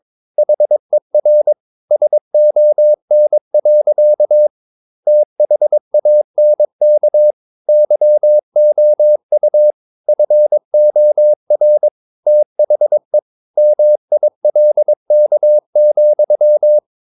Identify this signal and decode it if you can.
Morse code, heard as a keyed tone, about 22 wpm, 600 Hz. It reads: HER SON. THANK YOU FOR THE MILK,
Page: Romanen Never af Ken Follet på engelsk.